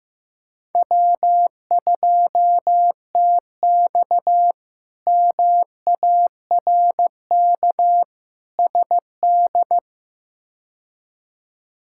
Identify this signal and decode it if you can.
Morse code, heard as a keyed tone, about 15 wpm, 700 Hz. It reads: W2TX MARK SD
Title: K1USN SST QSO Practice - 15wpm